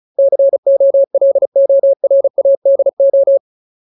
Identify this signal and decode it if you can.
Morse code, heard as a keyed tone, about 36 wpm, 550 Hz. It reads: COLORADO
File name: Ultimate Word List / 35WPM Ultimate